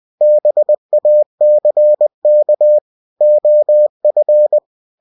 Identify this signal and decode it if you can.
Morse code, heard as a keyed tone, about 20 wpm, 600 Hz. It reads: BACK OF
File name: cw-books / winnie-the-pooh / winnie-20wpm-page-0000